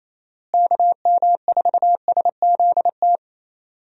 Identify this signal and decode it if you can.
Morse code, heard as a keyed tone, about 28 wpm, 700 Hz. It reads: KM4SZT
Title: USA Call Signs - 6 Characters - Repeated 3x - 28wpm